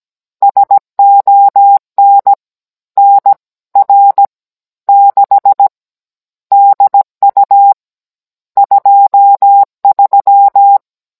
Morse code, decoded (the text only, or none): SON NR 6 DU 23